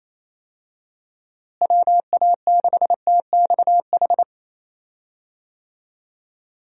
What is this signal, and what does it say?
Morse code, heard as a keyed tone, about 28 wpm, 700 Hz. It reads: WA6TXH
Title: USA Call Signs - 6 Characters - Repeated 2x - 28wpm